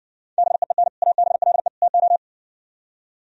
Morse code, heard as a keyed tone, about 69 wpm, 700 Hz. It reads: BEEN ABLE TO